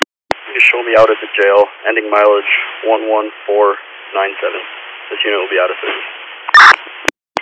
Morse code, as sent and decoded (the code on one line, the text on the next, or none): none